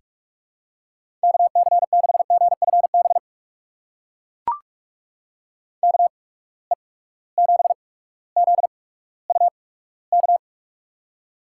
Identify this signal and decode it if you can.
Morse code, heard as a keyed tone, about 44 wpm, spaced as for 19 wpm, 700 Hz. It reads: KC6GFB KE7ZUK
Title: USA Call Signs - 6 Characters - 8x Character Spacing - 45wpm